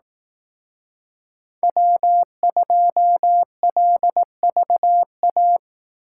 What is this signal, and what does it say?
Morse code, heard as a keyed tone, about 18 wpm, 700 Hz. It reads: W2LVA
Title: Speed Racing - USA 1x3 Call Signs 15wpm